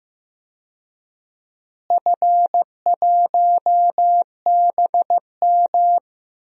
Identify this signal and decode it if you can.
Morse code, heard as a keyed tone, about 15 wpm, 700 Hz. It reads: F1BM
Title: DXCC-100 Call Signs - 4 Characters - Repeated 3x - 15wpm